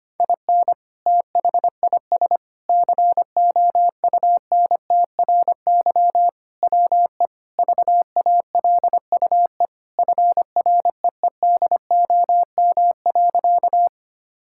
Morse code, heard as a keyed tone, about 25 wpm, 700 Hz. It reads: IN THIS COUNTRY WE VALUE FREEDOM.
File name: practice / 25wpm / Sentences with 3-Word Phrases - 25wpm